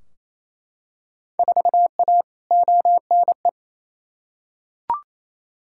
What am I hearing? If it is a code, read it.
Morse code, heard as a keyed tone, about 28 wpm, 700 Hz. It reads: 4A ONE
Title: Field Day Exchanges Encore - 28wpm